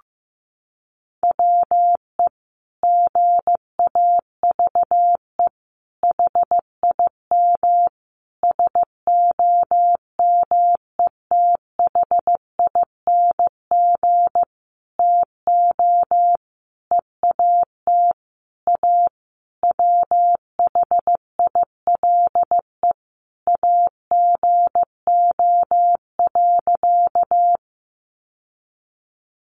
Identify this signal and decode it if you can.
Morse code, heard as a keyed tone, about 15 wpm, 700 Hz. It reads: WE GAVE HIM SOMETHING TO EAT A WHILE AGO.
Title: Sentences from Top 600 Words 15wpm